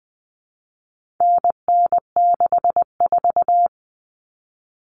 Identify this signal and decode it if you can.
Morse code, heard as a keyed tone, about 20 wpm, 700 Hz. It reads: NN64
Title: Common Cut Numbers - 4-Digit 20wpm